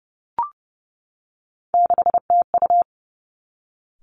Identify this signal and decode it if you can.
Morse code, heard as a keyed tone, about 30 wpm, 700 Hz. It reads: 6TU
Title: Three Challenging Letter-Number Combinations 30wpm